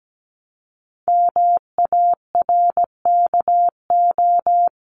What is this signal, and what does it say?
Morse code, heard as a keyed tone, about 17 wpm, 700 Hz. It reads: MARKO
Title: CWT Exchange 17wpm